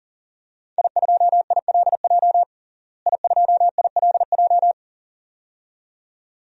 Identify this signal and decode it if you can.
Morse code, heard as a keyed tone, about 40 wpm, 700 Hz. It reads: I2ILJ I2ILJ
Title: DXCC-100 Call Signs - 5 Characters - Repeated 2x - 40wpm